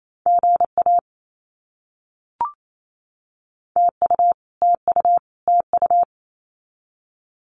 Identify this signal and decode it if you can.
Morse code, heard as a keyed tone, about 28 wpm, 700 Hz. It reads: GA TU TU TU
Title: Abbreviations Repeated 3x - 28wpm